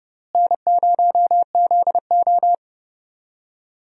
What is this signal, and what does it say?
Morse code, heard as a keyed tone, about 30 wpm, 700 Hz. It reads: N0ZO
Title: USA Call Signs - 4 Characters - Repeated 3x - 30wpm